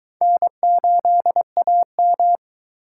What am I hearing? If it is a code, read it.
Morse code, heard as a keyed tone, about 22 wpm, 700 Hz. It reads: N8AM